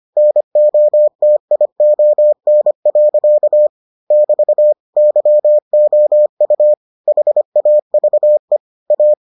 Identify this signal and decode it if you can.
Morse code, heard as a keyed tone, about 25 wpm, 600 Hz. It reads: NOTION. <BT> YOU HAVE A